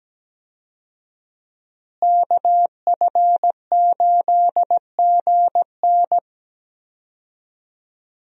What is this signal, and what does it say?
Morse code, heard as a keyed tone, about 17 wpm, 700 Hz. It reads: KF8GN
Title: USA Call Signs - 5 Characters - Repeated 2x - 17wpm